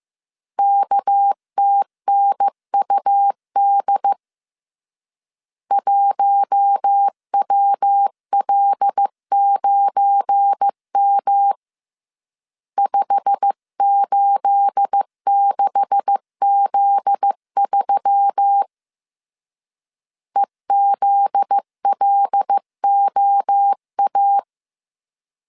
Morse code, heard as a keyed tone, about 15 wpm, 800 Hz. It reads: KTNUD 1WL9M 586Z3 EZLOA
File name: CW / DET / 1 9 1500C S A 14wpm 3222